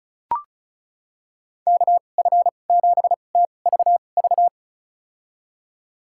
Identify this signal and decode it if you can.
Morse code, heard as a keyed tone, about 36 wpm, spaced as for 25 wpm, 690 Hz. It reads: KF7TVV
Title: USA Call Signs - 6 Characters - 2x Character Spacing - 35wpm